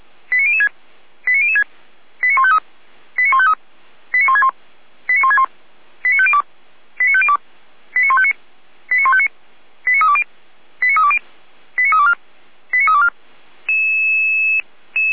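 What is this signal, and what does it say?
Five-tone selective call (ZVEI), heard as quick runs of five tones, tones of about 70 ms. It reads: 89006 89006 89153 89153 89161 89161 89502 89502 89159 89159 89320 89320 89324 89324
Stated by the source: ZVEI five-tone selective call 89160 received by SDR, archived on the feuerwehr-orlamuende fire brigade server.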